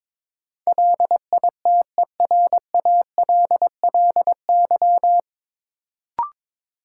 Morse code, heard as a keyed tone, about 21 wpm, 700 Hz. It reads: LITERALLY
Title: Sentences from Top 1400 Words - New Word Drill - 22wpm